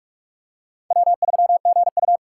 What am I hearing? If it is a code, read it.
Morse code, heard as a keyed tone, about 46 wpm, 700 Hz. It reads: W3GU
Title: USA Call Signs - 4 Characters - 3x Character Spacing - 45wpm